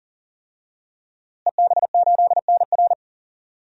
Morse code, heard as a keyed tone, about 40 wpm, 700 Hz. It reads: EB8NR
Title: DXCC-100 Call Signs - 5 Characters - Repeated 2x - 40wpm